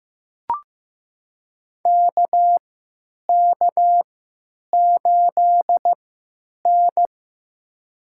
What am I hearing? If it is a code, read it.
Morse code, heard as a keyed tone, about 15 wpm, spaced as for 9 wpm, 700 Hz. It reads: KK8N
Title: USA Call Signs - 3x Character Spacing - 15wpm